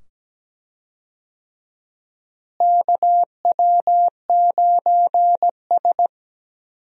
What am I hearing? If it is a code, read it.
Morse code, heard as a keyed tone, about 17 wpm, 700 Hz. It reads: KW9S
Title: USA Call Signs - 4 Characters - Repeated 2x - 17wpm